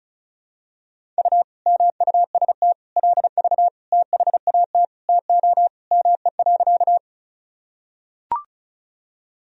Morse code, heard as a keyed tone, about 36 wpm, 700 Hz. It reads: U MUST LV THAT TO ME.
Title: Sentences from Top 400 Words and Abbreviations 35wpm